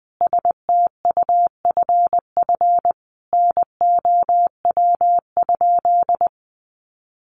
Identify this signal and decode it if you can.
Morse code, heard as a keyed tone, about 20 wpm, 700 Hz. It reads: STUFF NOW?
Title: Sentences from Top 1400 Words 20wpm